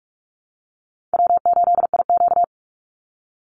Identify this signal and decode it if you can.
Morse code, heard as a keyed tone, about 44 wpm, 700 Hz. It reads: W8IQ
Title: Call Signs Repeated 2x - 45wpm